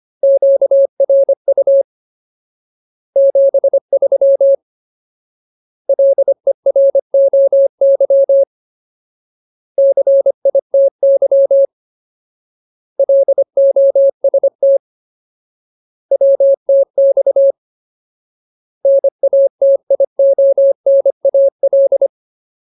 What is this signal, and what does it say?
Morse code, heard as a keyed tone, about 25 wpm, 550 Hz. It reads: QRU 73 LEROY CITY LOST WTX NATIONAL
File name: Ultimate Word List / 25WPM Ultimate